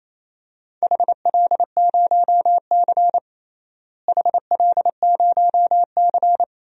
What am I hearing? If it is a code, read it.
Morse code, heard as a keyed tone, about 28 wpm, 700 Hz. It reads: HL0C HL0C
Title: DXCC-100 Call Signs - 4 Characters - Repeated 2x - 28wpm